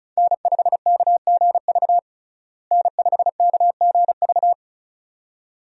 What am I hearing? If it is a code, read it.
Morse code, heard as a keyed tone, about 36 wpm, 700 Hz. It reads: N5KGV N5KGV
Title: USA Call Signs - 5 Characters - Repeated 2x - 35wpm